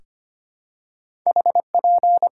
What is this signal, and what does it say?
Morse code, heard as a keyed tone, about 25 wpm, 700 Hz. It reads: HP
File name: Top 90 QSO Elements - 6x Character Spacing - 25wpm